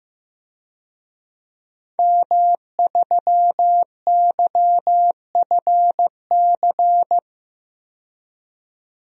Morse code, heard as a keyed tone, about 15 wpm, 700 Hz. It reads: M3YFC
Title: DXCC-100 Call Signs - 5 Characters - 15wpm